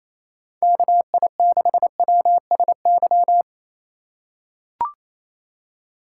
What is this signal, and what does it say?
Morse code, heard as a keyed tone, about 28 wpm, 700 Hz. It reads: KI6WSY
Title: USA Call Signs - 6 Characters - 4x Character Spacing - 28wpm